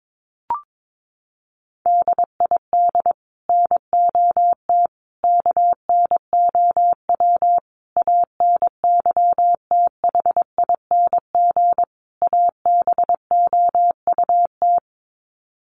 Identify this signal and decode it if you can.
Morse code, heard as a keyed tone, about 22 wpm, 700 Hz. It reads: DID NOT KNOW ANYTHING ABOUT
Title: Sets of 5 Words Encore2 22wpm